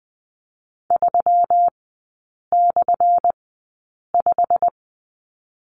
Morse code, heard as a keyed tone, about 19 wpm, spaced as for 8 wpm, 700 Hz. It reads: 3/5
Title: 178 Lesson 29e - Slash - Three Characters 20wpm